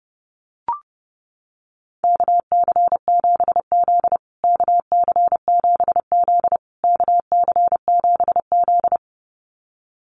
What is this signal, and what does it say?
Morse code, heard as a keyed tone, about 30 wpm, 700 Hz. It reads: KC7Z KC7Z KC7Z
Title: USA Call Signs Repeated 3x 30wpm